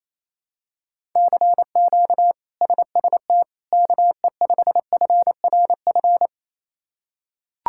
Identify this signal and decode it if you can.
Morse code, heard as a keyed tone, about 28 wpm, 700 Hz. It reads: CQ SST KE5FRF
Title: K1USN SST QSO Practice - 28wpm